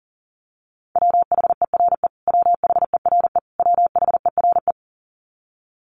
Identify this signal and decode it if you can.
Morse code, heard as a keyed tone, about 37 wpm, 700 Hz. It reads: WHERE WHERE WHERE
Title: Top 300 Words Repeated 40wpm